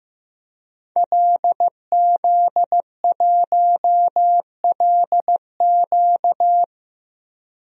Morse code, heard as a keyed tone, about 15 wpm, 700 Hz. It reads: LZ1LQ
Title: DXCC-100 Call Signs - 5 Characters - 15wpm